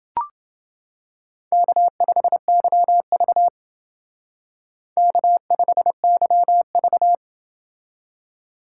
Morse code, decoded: K5YV K5YV

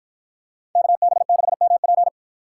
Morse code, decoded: KB6MP

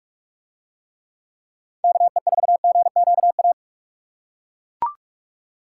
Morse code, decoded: KE4GQA

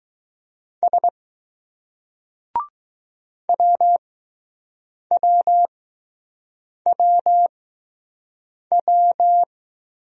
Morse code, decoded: S WWWW

